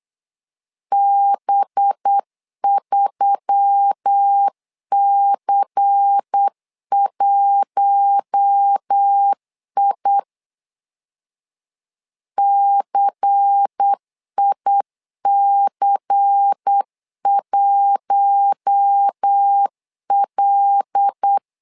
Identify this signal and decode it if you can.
Morse code, heard as a keyed tone, about 8 wpm, 800 Hz. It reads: B3C1I CIC1L
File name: LEZIONI / 5 LBVC / 5 LBVC 8wpm 2935